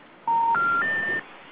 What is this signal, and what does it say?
Special information tone.